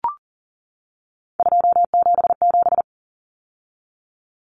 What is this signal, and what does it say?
Morse code, heard as a keyed tone, about 40 wpm, 700 Hz. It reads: 277
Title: Common Cut Numbers - 3-Digit 40wpm